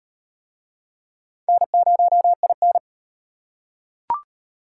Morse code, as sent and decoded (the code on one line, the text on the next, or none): -. ----- .. -.
N0IN